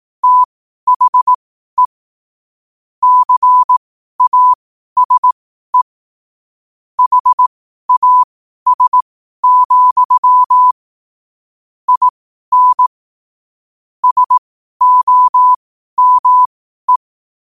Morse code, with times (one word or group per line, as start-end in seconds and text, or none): 0.2-1.9 THE
3.0-5.8 CASE
7.0-10.7 HAS,
11.9-12.9 IN
14.0-17.0 SOME